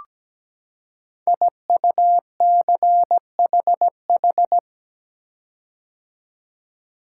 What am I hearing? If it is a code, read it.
Morse code, heard as a keyed tone, about 17 wpm, 700 Hz. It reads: IUCHH
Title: Code Groups - 5 Letters - 17wpm